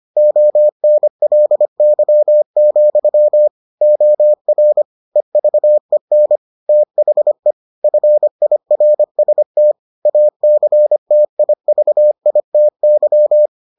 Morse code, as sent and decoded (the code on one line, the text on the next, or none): --- -. .-.. -.-- --..-- / --- .-. / . ...- . -. / - .... / ..-. .. .-. ... - / .- -.-. - .. ...- .. - -.--
ONLY, OR EVEN THE FIRST ACTIVITY